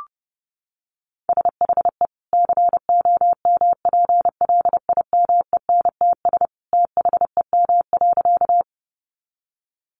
Morse code, heard as a keyed tone, about 30 wpm, 700 Hz. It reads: SHE COMPLIMENTS THEM.
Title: Easy Phrases 30wpm